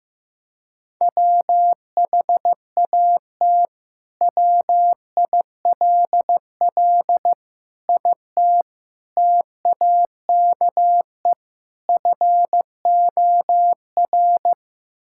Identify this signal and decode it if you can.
Morse code, heard as a keyed tone, about 15 wpm, 700 Hz. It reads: WHAT WILL IT TAKE FOR
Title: Sentences from Top 1600 Words 15wpm